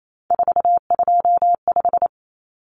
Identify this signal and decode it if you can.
Morse code, heard as a keyed tone, about 28 wpm, 700 Hz. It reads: 425
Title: Numbers - 3-Digit 28wpm